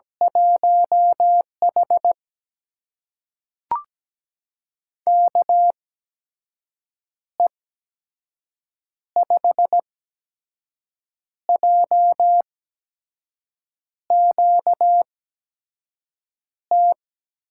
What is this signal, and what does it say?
Morse code, heard as a keyed tone, about 17 wpm, spaced as for 5 wpm, 700 Hz. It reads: 1H KE5JQT
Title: USA Call Signs - 8x Character Spacing - 17wpm